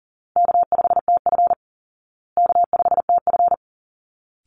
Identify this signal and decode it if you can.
Morse code, heard as a keyed tone, about 40 wpm, 700 Hz. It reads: K5TF K5TF
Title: Call Signs Repeated 2x - 40wpm